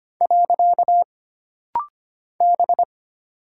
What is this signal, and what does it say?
Morse code, heard as a keyed tone, about 25 wpm, 700 Hz. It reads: . B
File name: Single Character (Letter:Number:Punctuation:Prosign) - Rapid-Fire - 25wpm